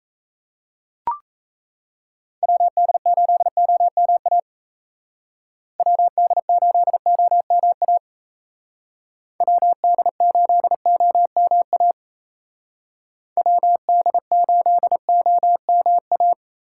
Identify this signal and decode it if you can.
Morse code, heard as a keyed tone, about 36 wpm, 700 Hz. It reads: WD8OMA WD8OMA WD8OMA WD8OMA